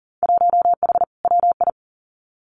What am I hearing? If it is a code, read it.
Morse code, heard as a keyed tone, about 41 wpm, 700 Hz. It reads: 1H WI
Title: Winter Field Day Exchanges 40wpm